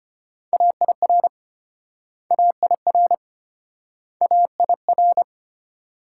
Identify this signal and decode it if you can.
Morse code, heard as a keyed tone, about 33 wpm, 700 Hz. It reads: AIR AIR AIR